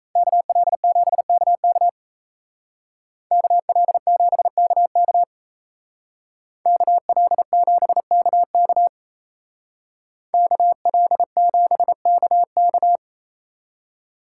Morse code, heard as a keyed tone, about 37 wpm, 700 Hz. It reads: KL7KK KL7KK KL7KK KL7KK